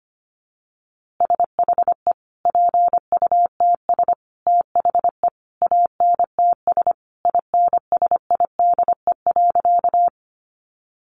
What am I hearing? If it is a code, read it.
Morse code, heard as a keyed tone, about 25 wpm, 700 Hz. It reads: SHE PUTS THE ANTS INSIDE.
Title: Easy Phrases 25wpm